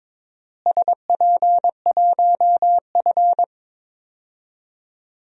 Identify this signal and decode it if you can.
Morse code, heard as a keyed tone, about 22 wpm, 700 Hz. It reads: SP1F
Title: DXCC-100 Call Signs - 4 Characters - Repeated 2x - 22wpm